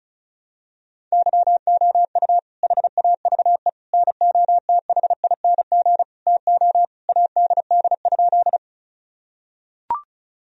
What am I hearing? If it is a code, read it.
Morse code, heard as a keyed tone, about 34 wpm, 700 Hz. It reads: YOU HAVE NOTHING TO ADD?
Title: Sentences from Top 1000 Words - Review Part 1 - 35wpm